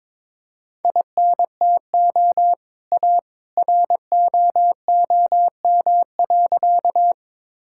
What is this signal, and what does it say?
Morse code, heard as a keyed tone, about 22 wpm, 700 Hz. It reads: INTO A ROOM.
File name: practice / 22wpm / Sentences with 2-Word Phrases - 22wpm